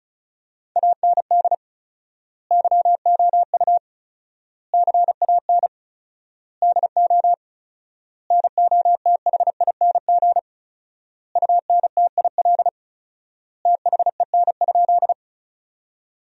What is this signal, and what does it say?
Morse code, heard as a keyed tone, about 34 wpm, 700 Hz. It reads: AND YOU CAN DO NOTHING UNTIL THEN?
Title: 4x Word-Spacing - Sentences from Top 500 Words - 35wpm